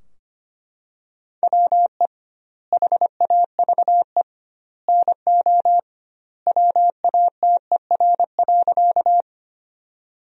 Morse code, heard as a keyed tone, about 25 wpm, 700 Hz. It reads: WE HAVE NO WATER.
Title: 2x Word-Spacing - Sentences from Top 100 Words - 25wpm